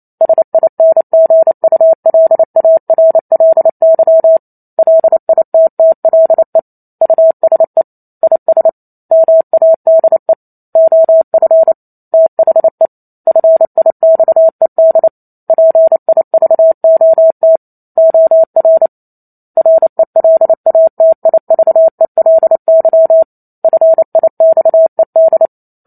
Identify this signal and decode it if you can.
Morse code, heard as a keyed tone, about 28 wpm, 650 Hz. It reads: SINGULARLY LITTLE USE IS MADE OF THE FIXED PIVOT OR RELATIVELY FIXED
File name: Chapter 2 - 28 WPM